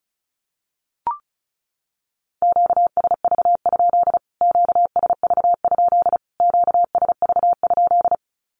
Morse code, heard as a keyed tone, about 34 wpm, 700 Hz. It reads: QSV? QSV? QSV?
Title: Q-Codes Repeated 3x 35wpm